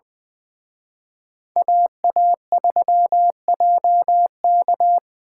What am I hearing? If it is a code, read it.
Morse code, heard as a keyed tone, about 24 wpm, 700 Hz. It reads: AA3JK
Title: Speed Racing - USA 2x2 Call Signs 20wpm